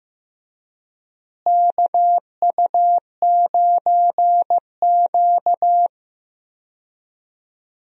Morse code, heard as a keyed tone, about 15 wpm, 700 Hz. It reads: KU9Q
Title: USA Call Signs - 4 Characters - Repeated 2x - 15wpm